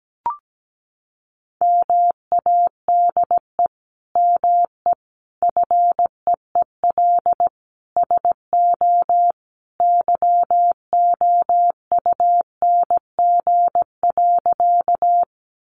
Morse code, heard as a keyed tone, about 17 wpm, 700 Hz. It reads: MADE ME FEEL SO YOUNG.
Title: Sentences from Top 400 Words 17wpm